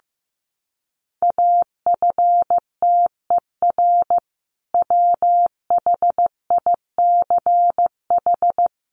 Morse code, heard as a keyed tone, about 15 wpm, 700 Hz. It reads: AFTER WHICH
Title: Sentences from Top 1400 Words 15wpm